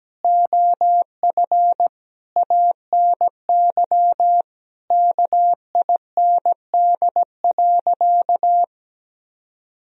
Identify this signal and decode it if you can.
Morse code, heard as a keyed tone, about 17 wpm, 700 Hz. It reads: OF ANY KIND.
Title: Sentences from Top 1000 Words - Review Part 4 - 17wpm